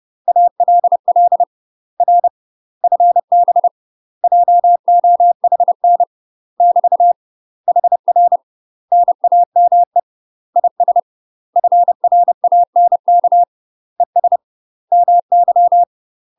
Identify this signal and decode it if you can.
Morse code, heard as a keyed tone, about 30 wpm, 700 Hz. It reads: ALL R FB JOHN <BT> HR NAME IS FRANK ES MY
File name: UK MASTER QSOs TRAINING / 30WPM / UK MASTER QSOs 0052